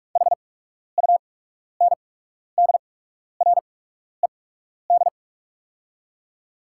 Morse code, heard as a keyed tone, about 45 wpm, spaced as for 12 wpm, 690 Hz. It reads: HUNDRED
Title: 8x Character Spacing - Top 500 Words - 45wpm